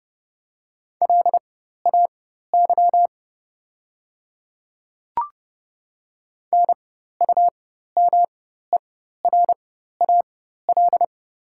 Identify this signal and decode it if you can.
Morse code, heard as a keyed tone, about 30 wpm, spaced as for 14 wpm, 700 Hz. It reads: LAY NUMERAL
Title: 4x Character Spacing - Top 500 Words - 30wpm